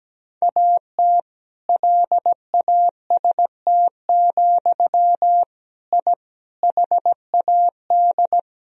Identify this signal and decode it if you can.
Morse code, heard as a keyed tone, about 17 wpm, 700 Hz. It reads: AT LAST, I HAD
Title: Sentences from Top 1000 Words - Review Part 2 - 17wpm